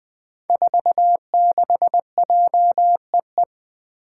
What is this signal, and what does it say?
Morse code, heard as a keyed tone, about 20 wpm, 700 Hz. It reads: 46JEE
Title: Five Challenging Letter-Number Combinations 20wpm